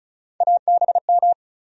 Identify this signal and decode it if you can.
Morse code, heard as a keyed tone, about 34 wpm, 700 Hz. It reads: ABM
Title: USA Call Sign Postfixes - 3 Letters 35wpm